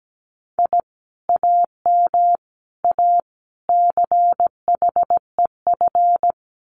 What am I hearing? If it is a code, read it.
Morse code, heard as a keyed tone, about 17 wpm, 700 Hz. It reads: I AM A CHEF
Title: Common QSO Phrases Encore 17wpm